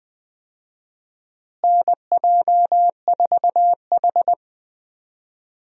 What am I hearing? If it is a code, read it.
Morse code, heard as a keyed tone, about 20 wpm, 700 Hz. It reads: NJ4H
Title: USA Call Signs - 4 Characters - Repeated 2x - 20wpm